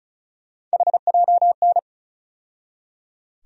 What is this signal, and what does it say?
Morse code, heard as a keyed tone, about 36 wpm, 700 Hz. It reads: HJN
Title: USA Call Sign Postfixes - 3 Letters 35wpm